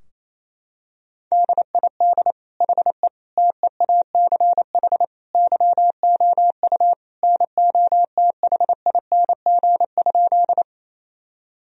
Morse code, heard as a keyed tone, about 28 wpm, 700 Hz. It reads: DID HE TEACH YOU NOTHING?
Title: Sentences from Top 1000 Words - Review Part 2 - 28wpm